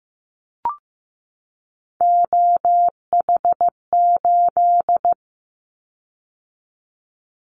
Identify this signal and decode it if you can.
Morse code, heard as a keyed tone, about 15 wpm, 700 Hz. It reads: OH8
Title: Callsign Prefixes for Top 25 DXCC - 15wpm